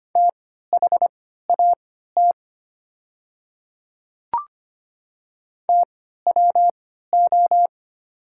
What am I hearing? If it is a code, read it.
Morse code, heard as a keyed tone, about 25 wpm, spaced as for 14 wpm, 700 Hz. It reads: THAT TWO
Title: 3x Character Spacing - Top 300 Words - 25wpm